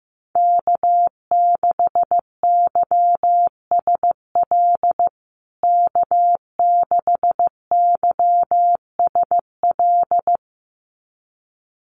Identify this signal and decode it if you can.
Morse code, heard as a keyed tone, about 15 wpm, 700 Hz. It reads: K6YSL K6YSL
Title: USA Call Signs Repeated 2x 15wpm